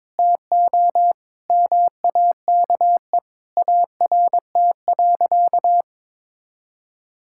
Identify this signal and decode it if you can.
Morse code, heard as a keyed tone, about 22 wpm, 700 Hz. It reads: TO MAKE ART.
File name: practice / 22wpm / Sentences with 2-Word Phrases - 22wpm